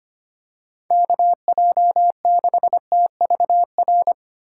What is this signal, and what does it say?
Morse code, heard as a keyed tone, about 25 wpm, 700 Hz. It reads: KJ6TVR
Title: USA Call Signs - 6 Characters - 3x Character Spacing - 25wpm